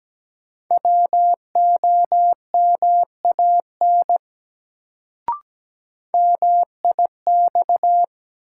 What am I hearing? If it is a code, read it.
Morse code, heard as a keyed tone, about 17 wpm, 700 Hz. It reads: WOMAN MIX
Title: Sentences from Top 1100 Words - New Word Drill Alt - 17wpm